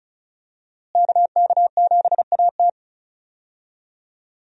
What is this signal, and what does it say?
Morse code, heard as a keyed tone, about 36 wpm, 700 Hz. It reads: KK7AT
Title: USA Call Signs - 5 Characters - Repeated 3x - 35wpm